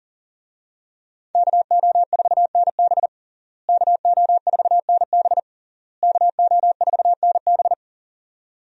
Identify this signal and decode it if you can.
Morse code, heard as a keyed tone, about 40 wpm, 700 Hz. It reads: KO4NB KO4NB KO4NB
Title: USA Call Signs - 5 Characters - Repeated 3x - 40wpm